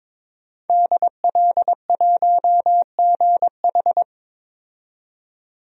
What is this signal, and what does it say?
Morse code, heard as a keyed tone, about 22 wpm, 700 Hz. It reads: DL1GH
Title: DXCC-100 Call Signs - 5 Characters - 22wpm